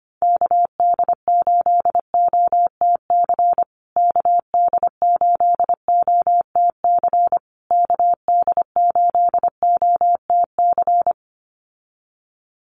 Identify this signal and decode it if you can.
Morse code, heard as a keyed tone, about 25 wpm, 700 Hz. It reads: KD8OTC KD8OTC KD8OTC
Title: USA Call Signs Repeated 3x 25wpm